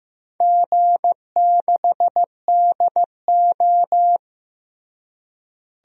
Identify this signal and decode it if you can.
Morse code, heard as a keyed tone, about 15 wpm, 700 Hz. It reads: G6DO
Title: DXCC-100 Call Signs - 4 Characters - Repeated 2x - 15wpm